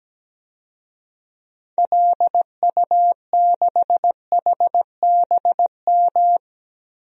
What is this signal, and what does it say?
Morse code, heard as a keyed tone, about 17 wpm, 700 Hz. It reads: LU6HBM
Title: DXCC-100 Call Signs - 6 Characters - Repeated 2x - 17wpm